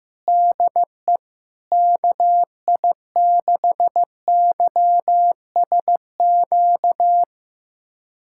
Morse code, heard as a keyed tone, about 16 wpm, 700 Hz. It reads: DE KI6YSQ
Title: USA Callsign DE Callsign - 6 Characters - 15wpm